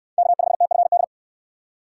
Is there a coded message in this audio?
Yes: Morse code from 0.2 to 1.0 s, heard as a keyed tone, about 66 wpm, 690 Hz. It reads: K5TFD